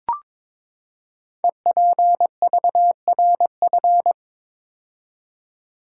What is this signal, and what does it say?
Morse code, heard as a keyed tone, about 22 wpm, 700 Hz. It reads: EPVRF
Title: Code Groups - 5 Letters - 22wpm